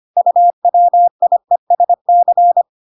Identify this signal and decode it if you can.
Morse code, heard as a keyed tone, about 25 wpm, 700 Hz. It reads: UWIESC